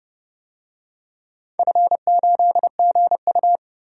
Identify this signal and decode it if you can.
Morse code, heard as a keyed tone, about 30 wpm, 700 Hz. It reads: F8GU
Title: DXCC-100 Call Signs - 4 Characters - 30wpm